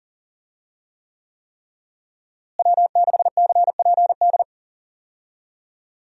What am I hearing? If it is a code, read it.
Morse code, heard as a keyed tone, about 40 wpm, 700 Hz. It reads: W6CPD
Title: USA Call Signs - 5 Characters - Repeated 3x - 40wpm